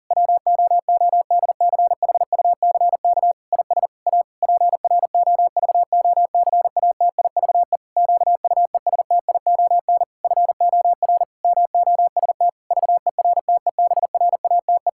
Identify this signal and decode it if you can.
Morse code, heard as a keyed tone, about 40 wpm, 700 Hz. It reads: WOODCHUCK IS A PROVOCATIVE QUESTION FOR MOST VERTEBRATE
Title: Pangram Challenge v2 - 40wpm